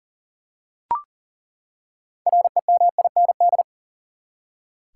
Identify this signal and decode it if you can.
Morse code, heard as a keyed tone, about 37 wpm, 700 Hz. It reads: REMIND
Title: Six Letter Words - 40wpm